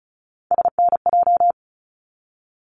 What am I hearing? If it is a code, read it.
Morse code, heard as a keyed tone, about 36 wpm, 700 Hz. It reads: SNJ
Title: ARRL Sections 35wpm